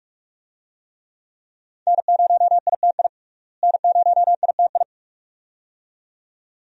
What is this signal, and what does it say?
Morse code, heard as a keyed tone, about 46 wpm, 700 Hz. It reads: N0ITI N0ITI
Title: USA Call Signs - 5 Characters - Repeated 2x - 45wpm